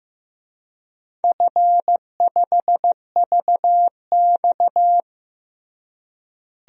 Morse code, heard as a keyed tone, about 16 wpm, 700 Hz. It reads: F5VX
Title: DXCC-100 Call Signs - 4 Characters - Repeated 3x - 15wpm